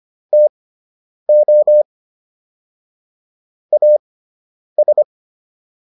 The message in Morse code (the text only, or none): TO AS